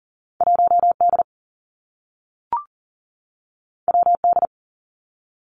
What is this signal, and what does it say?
Morse code, heard as a keyed tone, about 38 wpm, 700 Hz. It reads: 1 D W D
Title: Two Challenging Letter-Number Combinations 40wpm